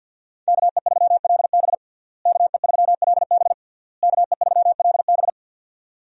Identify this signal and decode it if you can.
Morse code, heard as a keyed tone, about 46 wpm, 690 Hz. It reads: KE3LB KE3LB KE3LB
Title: USA Call Signs - 5 Characters - Repeated 3x - 50wpm